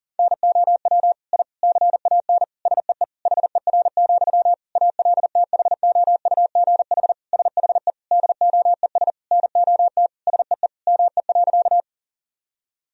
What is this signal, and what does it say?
Morse code, heard as a keyed tone, about 40 wpm, 700 Hz. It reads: NOW I CAN SEE HER, ALTHOUGH SHE DOES NOT SEE ME.
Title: Sentences from Top 1000 Words - Review Part 3 - 40wpm